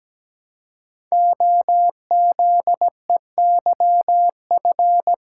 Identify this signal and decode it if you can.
Morse code, heard as a keyed tone, about 17 wpm, 700 Hz. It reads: OZEYF
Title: Code Groups - 5 Letter:Numbers - 17wpm